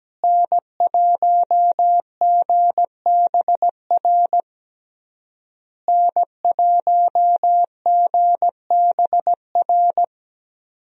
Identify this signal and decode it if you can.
Morse code, heard as a keyed tone, about 17 wpm, 700 Hz. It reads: N1GBR N1GBR
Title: USA Call Signs - 5 Characters - Repeated 3x - 17wpm